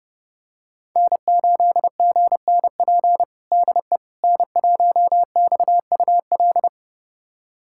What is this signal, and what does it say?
Morse code, heard as a keyed tone, about 30 wpm, 700 Hz. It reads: N8GNP DE N1XUL
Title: USA Callsign DE Callsign - 5 Characters - 30wpm